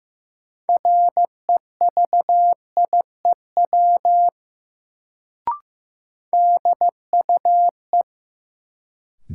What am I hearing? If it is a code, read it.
Morse code, heard as a keyed tone, about 15 wpm, 700 Hz. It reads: REVIEW DUE
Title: Sentences from Top 900 Words - New Word Drill - 15wpm